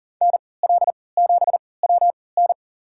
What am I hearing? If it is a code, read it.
Morse code, heard as a keyed tone, about 40 wpm, spaced as for 23 wpm, 700 Hz. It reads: NL7WN
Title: USA Call Signs - 5 Characters - 3x Character Spacing - 40wpm